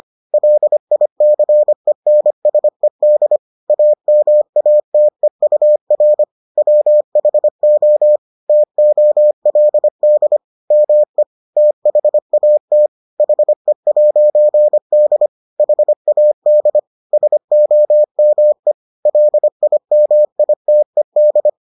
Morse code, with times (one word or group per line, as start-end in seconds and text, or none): 0.3-3.4 LICENSED
3.7-6.2 AMATEUR
6.6-8.2 WHO
8.5-10.4 TOLD
10.7-11.2 ME
11.6-12.9 THAT
13.2-15.3 HE'D
15.6-16.8 HAD
17.1-18.7 SOME
19.1-21.5 LIMITED